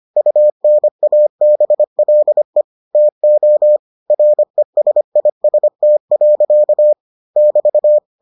Morse code, heard as a keyed tone, about 25 wpm, 600 Hz. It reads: UNABLE TO RESIST. <BT>